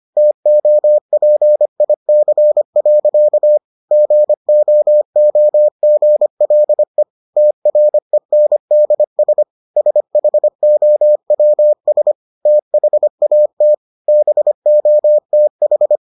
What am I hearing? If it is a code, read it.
Morse code, heard as a keyed tone, about 25 wpm, 600 Hz. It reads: TOPIC. GOOGLE TRENDS SHOWS THAT BOTH